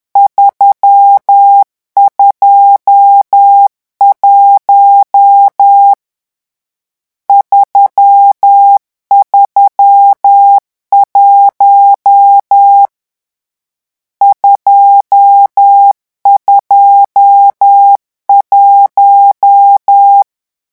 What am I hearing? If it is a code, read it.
Morse code, heard as a keyed tone, about 10 wpm, 800 Hz. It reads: 321 331 221
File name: morse mp3 / lecon10-vitesse 10